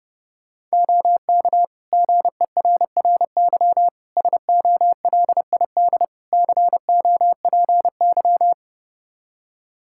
Morse code, heard as a keyed tone, about 30 wpm, 700 Hz. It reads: OK GERRY SOLID COPY
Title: Practice QSO - Exchange 3 - More Info - 30wpm